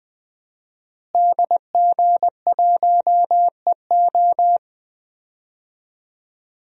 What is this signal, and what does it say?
Morse code, heard as a keyed tone, about 20 wpm, 700 Hz. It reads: DG1EO
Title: DXCC-100 Call Signs - 5 Characters - Repeated 3x - 20wpm